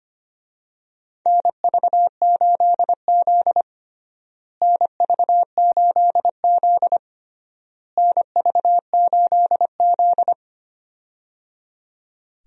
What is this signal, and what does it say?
Morse code, heard as a keyed tone, about 25 wpm, 700 Hz. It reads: NV8Z NV8Z NV8Z
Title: USA Call Signs - 4 Characters - Repeated 3x - 25wpm